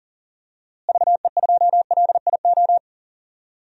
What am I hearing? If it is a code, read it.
Morse code, heard as a keyed tone, about 40 wpm, 700 Hz. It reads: VE2LIO
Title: DXCC-100 Call Signs - 6 Characters - Repeated 2x - 40wpm